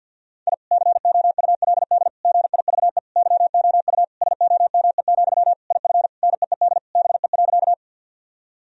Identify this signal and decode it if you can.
Morse code, heard as a keyed tone, about 48 wpm, 700 Hz. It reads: I COULD GIVE YOU SOME, IF NEED BE.
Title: Sentences from Top 1000 Words - Review Part 1 - 50wpm